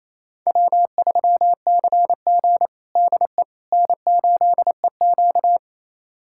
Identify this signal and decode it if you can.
Morse code, heard as a keyed tone, about 28 wpm, 700 Hz. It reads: W3CG DE N8EQ
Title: USA Callsign DE Callsign - 4 Characters - 28wpm